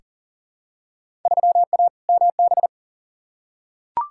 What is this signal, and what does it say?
Morse code, heard as a keyed tone, about 40 wpm, 700 Hz. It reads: 3A MB
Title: Field Day Exchanges Encore - 40wpm